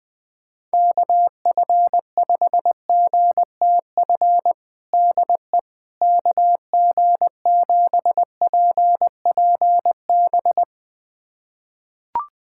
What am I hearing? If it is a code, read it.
Morse code, heard as a keyed tone, about 20 wpm, 700 Hz. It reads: KF5GTF DE KG7PPB
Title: USA Callsign DE Callsign 20wpm